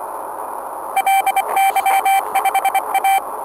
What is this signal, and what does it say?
Morse code, heard as a keyed tone, about 24 wpm, 780 Hz, with noise. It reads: LY5A